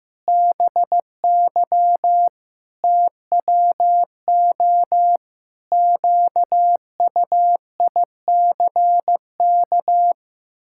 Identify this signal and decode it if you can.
Morse code, heard as a keyed tone, about 15 wpm, 700 Hz. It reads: BY TWO QUICK
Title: Pangram Challenge v2 - 15wpm